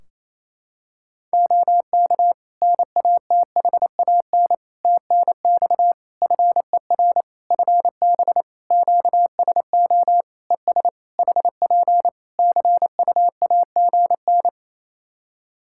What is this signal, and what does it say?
Morse code, heard as a keyed tone, about 28 wpm, 700 Hz. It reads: OK NATHAN TNX FER FB QSO ES HP CUAGN
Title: Practice QSO - Exchange 4 - Ending - 28wpm